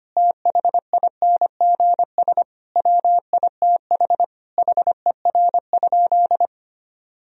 Morse code, heard as a keyed tone, about 25 wpm, 700 Hz. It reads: THINGS WITH HER?
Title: Sentences from Top 1000 Words - Review Part 4 - 25wpm